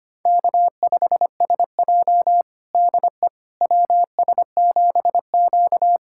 Morse code, heard as a keyed tone, about 25 wpm, 700 Hz. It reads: K5SJ DE WS7Q
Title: USA Callsign DE Callsign - 4 Characters - 25wpm